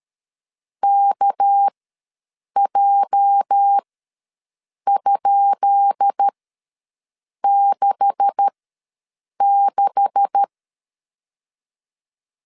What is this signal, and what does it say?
Morse code, heard as a keyed tone, about 13 wpm, spaced as for 6 wpm, 800 Hz. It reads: KJ?66
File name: LEZIONI / 10 SIGNS / 10B INT ws 12wpm 3527